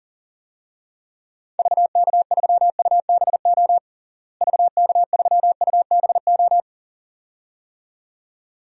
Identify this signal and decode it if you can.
Morse code, heard as a keyed tone, about 40 wpm, 700 Hz. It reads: VK3UBO VK3UBO
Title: DXCC-100 Call Signs - 6 Characters - Repeated 2x - 40wpm